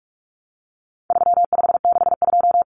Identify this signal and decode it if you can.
Morse code, heard as a keyed tone, about 44 wpm, 700 Hz. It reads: 3562
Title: Numbers - 4-Digit 45wpm